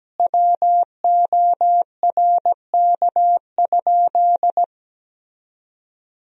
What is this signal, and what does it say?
Morse code, heard as a keyed tone, about 17 wpm, 700 Hz. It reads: WORK?
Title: Sentences from Top 1000 Words - Review Part 4 - 17wpm